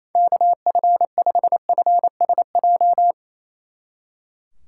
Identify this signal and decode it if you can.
Morse code, heard as a keyed tone, about 33 wpm, 700 Hz. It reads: KF5FSJ
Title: Speed Racing - USA 2x3 Call Signs 28wpm